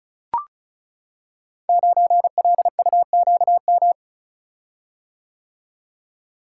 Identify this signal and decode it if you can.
Morse code, heard as a keyed tone, about 36 wpm, 700 Hz. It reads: 9LUQM
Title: Code Groups - 5 Letter:Numbers - 35wpm